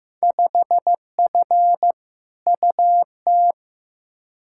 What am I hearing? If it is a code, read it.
Morse code, heard as a keyed tone, about 15 wpm, 700 Hz. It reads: 5F UT
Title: Field Day Exchanges - 15wpm